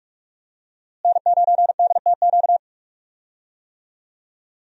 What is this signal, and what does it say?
Morse code, heard as a keyed tone, about 46 wpm, 700 Hz. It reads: N9DTQ
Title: USA Call Signs - 5 Characters - Repeated 3x - 45wpm